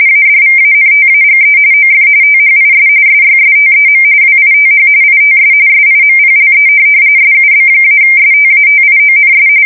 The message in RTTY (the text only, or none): ADVANTAGE OF THIS SYSTEM WITH ITS SIMPLE SUB-STATIONS